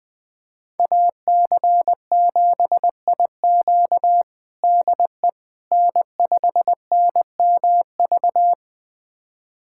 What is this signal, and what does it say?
Morse code, heard as a keyed tone, about 21 wpm, 700 Hz. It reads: AC7IQ DE N5NMV
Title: USA Callsign DE Callsign - 5 Characters - 20wpm